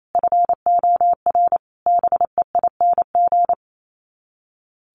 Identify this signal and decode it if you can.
Morse code, heard as a keyed tone, about 28 wpm, 700 Hz. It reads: FOR BEING
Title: Sets of 5 Words Encore 28wpm